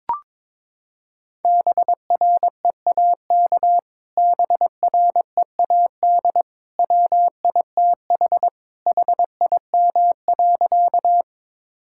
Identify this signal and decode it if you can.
Morse code, heard as a keyed tone, about 22 wpm, 700 Hz. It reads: BREAK BREAD WITH HIM.
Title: Sentences from Top 1600 Words 22wpm